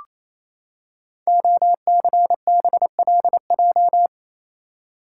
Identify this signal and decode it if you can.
Morse code, heard as a keyed tone, about 27 wpm, 700 Hz. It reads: OCBLJ
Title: Code Groups - 5 Letters - 28wpm